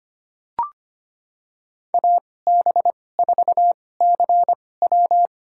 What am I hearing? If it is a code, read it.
Morse code, heard as a keyed tone, about 25 wpm, spaced as for 18 wpm, 700 Hz. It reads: AB4CW
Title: USA Call Signs - 5 Characters - 2x Character Spacing - 25wpm